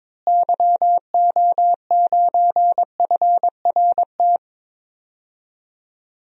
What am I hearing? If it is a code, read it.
Morse code, heard as a keyed tone, about 22 wpm, 700 Hz. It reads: YO9FRT
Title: DXCC-100 Call Signs - 6 Characters - Repeated 3x - 22wpm